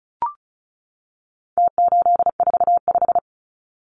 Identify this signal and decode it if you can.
Morse code, heard as a keyed tone, about 34 wpm, 700 Hz. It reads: T845
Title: Numbers - Contest Style - 4-Digit 35wpm